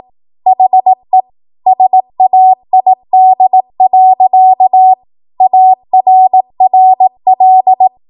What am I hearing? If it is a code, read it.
Morse code, heard as a keyed tone, about 18 wpm, 750 Hz. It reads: HE SAID. ARRL
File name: Archive / 18 WPM / 18WPM